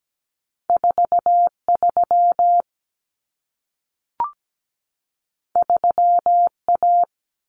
Morse code, heard as a keyed tone, about 17 wpm, 700 Hz. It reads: 43 3A
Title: Common Cut Numbers - 2-Digit 17wpm